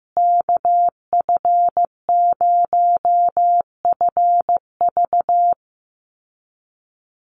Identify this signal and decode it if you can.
Morse code, heard as a keyed tone, about 14 wpm, 700 Hz. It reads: KF0FV
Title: USA Call Signs Repeated 3x 15wpm